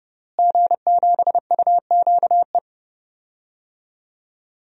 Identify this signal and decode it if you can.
Morse code, heard as a keyed tone, about 32 wpm, 700 Hz. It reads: G7UQE
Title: DXCC-100 Call Signs - 5 Characters - Repeated 2x - 30wpm